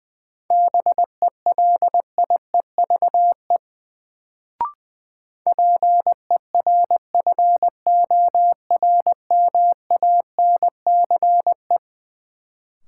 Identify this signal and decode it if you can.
Morse code, heard as a keyed tone, about 20 wpm, 700 Hz. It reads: BELIEVE PERFORMANCE
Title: Sentences from Top 1000 Words - New Word Drill - 20wpm